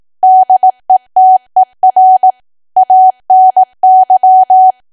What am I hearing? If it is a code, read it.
Morse code, heard as a keyed tone, about 18 wpm, 750 Hz. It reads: DETER ANY
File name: cwa / W1AW-Level2 / 18WPM